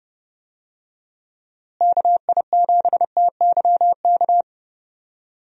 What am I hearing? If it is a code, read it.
Morse code, heard as a keyed tone, about 31 wpm, 700 Hz. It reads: KI7TYK
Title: USA Call Signs - 6 Characters - Repeated 2x - 30wpm